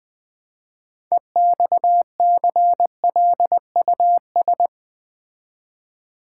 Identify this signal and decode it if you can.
Morse code, heard as a keyed tone, about 20 wpm, 700 Hz. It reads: EXCLUS
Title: Build A Word - 3 Letters At A Time - 9-Letter Words - 20wpm